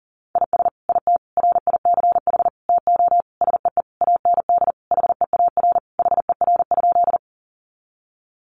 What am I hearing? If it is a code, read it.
Morse code, heard as a keyed tone, about 40 wpm, 700 Hz. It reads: IS IT RICH TO SEE AND HEAR HER?
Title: Sentences from Top 1100 Words 40wpm